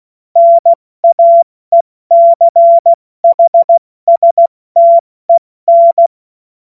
Morse code, heard as a keyed tone, about 16 wpm, spaced as for 14 wpm, 680 Hz. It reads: NAECHSTEN
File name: Morsekurs / Witze / Witz 27 Mai 2025 70 F80